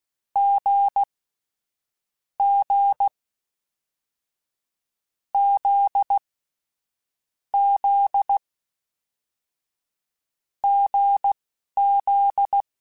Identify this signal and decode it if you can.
Morse code, heard as a keyed tone, about 16 wpm, spaced as for 11 wpm, 800 Hz. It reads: G G Z Z GZ
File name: Difficult Combos / Difficult Combos-mp3s / awj1 ndb6 gz7 @10